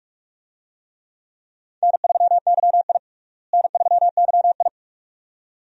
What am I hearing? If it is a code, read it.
Morse code, heard as a keyed tone, about 45 wpm, 700 Hz. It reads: N3YI N3YI
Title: USA Call Signs - 4 Characters - Repeated 2x - 45wpm